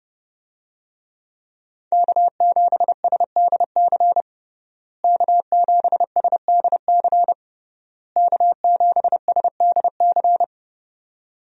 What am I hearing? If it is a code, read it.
Morse code, heard as a keyed tone, about 32 wpm, 700 Hz. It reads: K7SDC K7SDC K7SDC
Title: USA Call Signs - 5 Characters - Repeated 3x - 30wpm